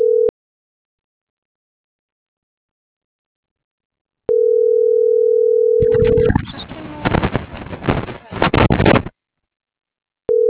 Ringback tone (North America), 440 + 480 Hz.